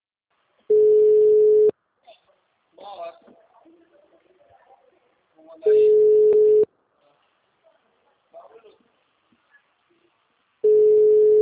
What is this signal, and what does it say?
Ringback tone (Europe, 425 Hz), with speech.
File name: out- -2009- - - .194